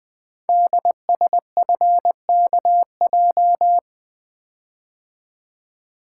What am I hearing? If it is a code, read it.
Morse code, heard as a keyed tone, about 20 wpm, 700 Hz. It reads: DSFKJ
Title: Code Groups - 5 Letters - 20wpm